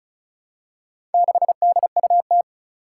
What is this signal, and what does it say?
Morse code, heard as a keyed tone, about 36 wpm, 700 Hz. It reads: 6DUT
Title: Four Challenging Letter-Number Combinations 35wpm